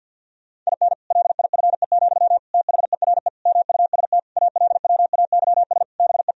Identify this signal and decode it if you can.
Morse code, heard as a keyed tone, about 50 wpm, 700 Hz. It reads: IN LIFE, THERE MUST ALWAYS BE
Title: Sentences from Top 1000 Words - Review Part 4 - 50wpm